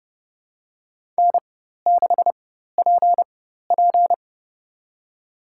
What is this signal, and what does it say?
Morse code, heard as a keyed tone, about 30 wpm, spaced as for 14 wpm, 700 Hz. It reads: N6PP
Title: USA Call Signs - 4 Characters - 4x Character Spacing - 30wpm